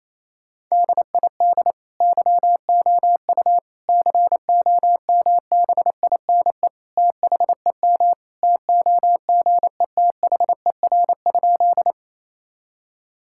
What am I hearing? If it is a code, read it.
Morse code, heard as a keyed tone, about 28 wpm, 700 Hz. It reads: DID YOU COMBINE THEM TOGETHER?
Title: Sentences from Top 1600 Words 28wpm